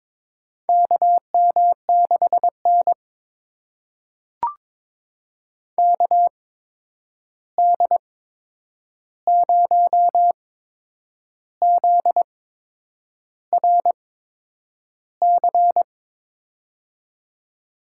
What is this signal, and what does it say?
Morse code, heard as a keyed tone, about 22 wpm, spaced as for 8 wpm, 700 Hz. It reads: KM6N KD0ZRC
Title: USA Call Signs - 8x Character Spacing - 22wpm